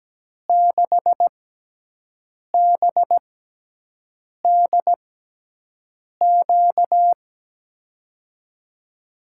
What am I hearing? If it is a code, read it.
Morse code, heard as a keyed tone, about 17 wpm, spaced as for 6 wpm, 700 Hz. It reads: 6BDQ
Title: USA Call Signs - 6 Characters - 6x Character Spacing - 17wpm